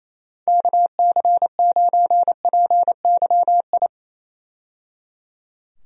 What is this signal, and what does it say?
Morse code, heard as a keyed tone, about 27 wpm, 700 Hz. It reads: KC9PYI
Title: USA Call Signs - 6 Characters - 28wpm